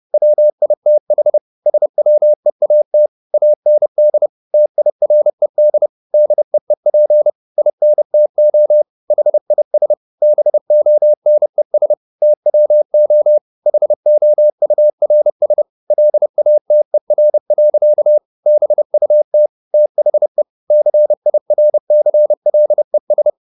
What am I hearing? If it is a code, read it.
Morse code, heard as a keyed tone, about 30 wpm, 600 Hz. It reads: WITH SWEAT AND TIRED DEEP INTO HIS BONES TWO HOURS LATER. BUT THE CIRCLES